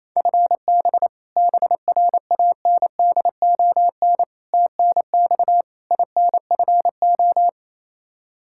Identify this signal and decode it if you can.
Morse code, heard as a keyed tone, about 27 wpm, 700 Hz. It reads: FB BRANDON TNX INFO
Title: Practice QSO - Exchange 3 - More Info - 28wpm